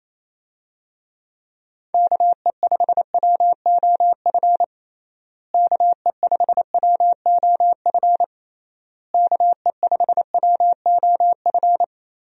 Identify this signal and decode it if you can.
Morse code, heard as a keyed tone, about 28 wpm, 700 Hz. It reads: KE5WOF KE5WOF KE5WOF
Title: USA Call Signs - 6 Characters - Repeated 3x - 28wpm